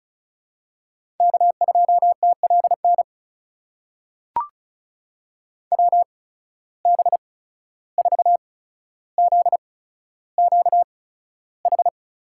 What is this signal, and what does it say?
Morse code, heard as a keyed tone, about 36 wpm, spaced as for 14 wpm, 700 Hz. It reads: K2TLN WB4ZQH
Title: USA Call Signs - 8x Character Spacing - 35wpm